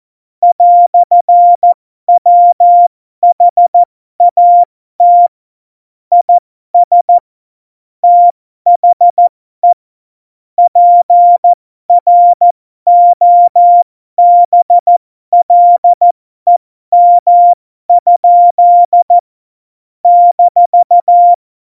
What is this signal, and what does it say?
Morse code, heard as a keyed tone, about 14 wpm, spaced as for 12 wpm, 700 Hz. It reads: "WHAT IS THE PROBLEM? -
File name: Morsekurs / Witze